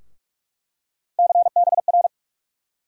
Morse code, heard as a keyed tone, about 46 wpm, 700 Hz. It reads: CBR
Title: USA Call Sign Postfixes - 3 Letters 45wpm